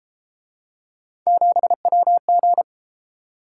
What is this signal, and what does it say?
Morse code, heard as a keyed tone, about 31 wpm, 700 Hz. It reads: 7WG